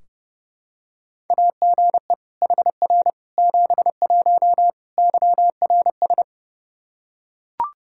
Morse code, heard as a keyed tone, about 30 wpm, 700 Hz. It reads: AGE HR 71 YRS
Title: Practice QSO - Exchange 3 - More Info - 30wpm